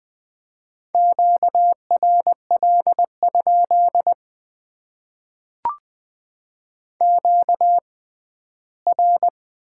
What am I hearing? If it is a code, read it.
Morse code, heard as a keyed tone, about 20 wpm, spaced as for 14 wpm, 700 Hz. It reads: QRL? QR
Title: Top 90 QSO Elements - 6x Character Spacing - 20wpm